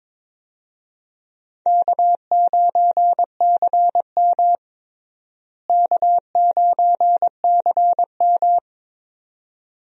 Morse code, heard as a keyed tone, about 21 wpm, 700 Hz. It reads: K9CM K9CM
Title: USA Call Signs - 4 Characters - Repeated 2x - 22wpm